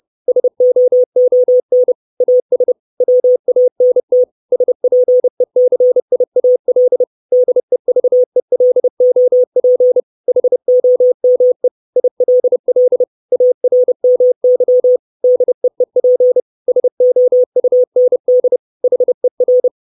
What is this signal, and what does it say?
Morse code, heard as a keyed tone, about 30 wpm, 500 Hz. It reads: SOON AS WANT SPECIAL DEVELOP HOME ILL ARMY DEEP SOUND HER
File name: 632 Words 30WPM